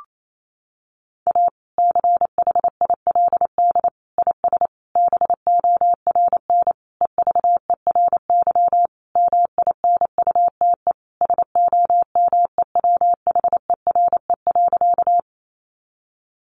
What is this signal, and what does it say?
Morse code, heard as a keyed tone, about 28 wpm, 700 Hz. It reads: A CHILD IS BORN EVERY MINUTE SOMEWHERE.
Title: Sentences from Top 1400 Words 28wpm